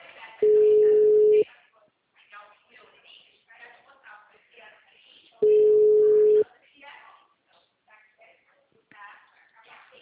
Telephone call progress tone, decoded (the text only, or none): ringback tone